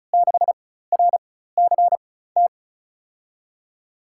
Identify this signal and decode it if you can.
Morse code, heard as a keyed tone, about 36 wpm, spaced as for 16 wpm, 700 Hz. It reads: 6RCT
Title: USA Call Signs - 6 Characters - 4x Character Spacing - 35wpm